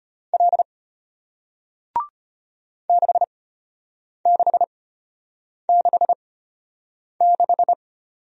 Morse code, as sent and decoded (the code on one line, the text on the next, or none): .-.. / -.... -.... -.... -....
L 6666